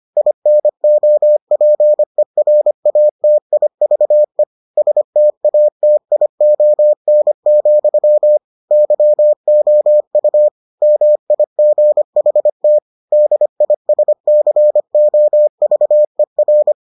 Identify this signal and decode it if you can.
Morse code, heard as a keyed tone, about 25 wpm, 600 Hz. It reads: INOPERATIVE STATION, YOU MIGHT DISCOVER